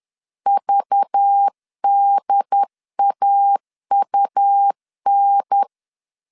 Morse code, heard as a keyed tone, about 10 wpm, 800 Hz. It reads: VDAUN